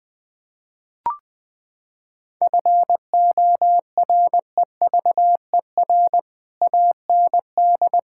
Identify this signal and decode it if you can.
Morse code, heard as a keyed tone, about 20 wpm, 700 Hz. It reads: FOREVER AND
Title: Sentences from Top 1600 Words 20wpm